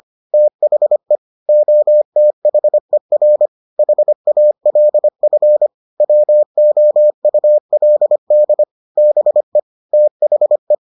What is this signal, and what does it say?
Morse code, heard as a keyed tone, about 25 wpm, 600 Hz. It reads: THE OTHER HALF WOULD BE THE